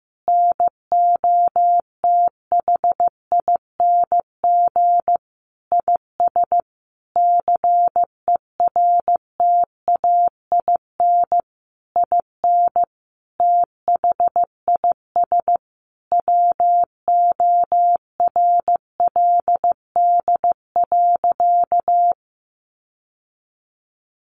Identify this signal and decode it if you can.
Morse code, heard as a keyed tone, about 14 wpm, 700 Hz. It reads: NOTHING IS CERTAIN IN THIS WORLD.
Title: Sentences from Top 600 Words 15wpm